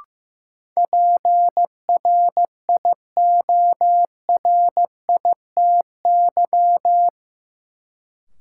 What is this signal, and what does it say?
Morse code, heard as a keyed tone, about 15 wpm, 700 Hz. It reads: PRIORITY
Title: Sentences from Top 1600 Words - New Word Drill Alt - 15wpm